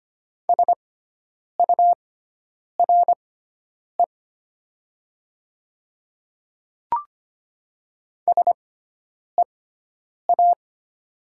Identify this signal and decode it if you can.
Morse code, heard as a keyed tone, about 25 wpm, spaced as for 9 wpm, 700 Hz. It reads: SURE SEA
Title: 6x Character Spacing - Top 300 Words - 25wpm